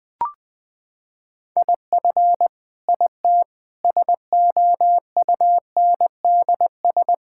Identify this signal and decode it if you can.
Morse code, heard as a keyed tone, about 20 wpm, 700 Hz. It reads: IF IT SOUNDS
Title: Sentences with 9-Word Phrases - 20wpm